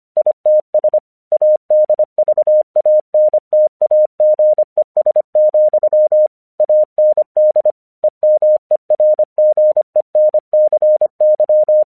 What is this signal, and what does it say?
Morse code, heard as a keyed tone, about 25 wpm, 610 Hz. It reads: ITS ADVANTAGES, AND EMERGENCY
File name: 1 English 25wpm / AQC05